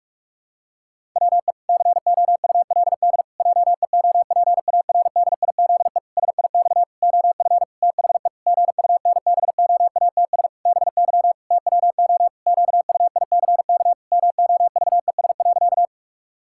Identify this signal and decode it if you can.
Morse code, heard as a keyed tone, about 46 wpm, 700 Hz. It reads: WE COULD JEOPARDIZE SIX OF THE GUNBOATS BY TWO QUICK MOVES.
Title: Pangram Challenge v2 - 45wpm